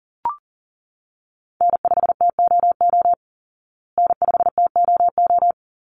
Morse code, heard as a keyed tone, about 37 wpm, 700 Hz. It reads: N5TOO N5TOO
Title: Call Signs Repeated 3x - 40wpm